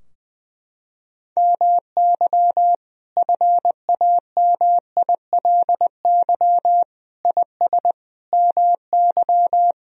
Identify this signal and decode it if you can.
Morse code, heard as a keyed tone, about 20 wpm, 700 Hz. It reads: MY FAMILY IS MY
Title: Sentences from Top 1000 Words - Review Part 4 - 20wpm